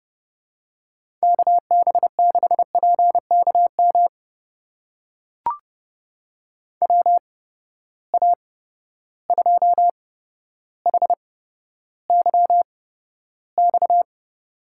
Morse code, heard as a keyed tone, about 31 wpm, spaced as for 13 wpm, 700 Hz. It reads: KB6PKM WA2HYX
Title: USA Call Signs - 6 Characters - 8x Character Spacing - 30wpm